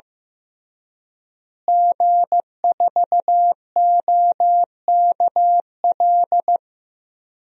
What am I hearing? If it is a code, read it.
Morse code, heard as a keyed tone, about 16 wpm, 700 Hz. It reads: G4OKL
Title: DXCC-100 Call Signs - 5 Characters - Repeated 3x - 15wpm